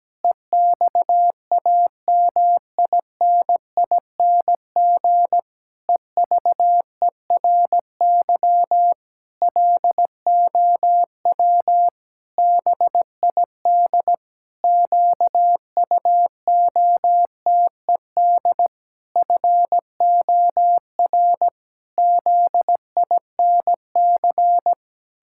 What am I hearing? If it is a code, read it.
Morse code, heard as a keyed tone, about 17 wpm, 700 Hz. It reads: EXAMINING EVERY LOW BID QUOTED FOR ZINC
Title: Pangram Challenge v2 - 17wpm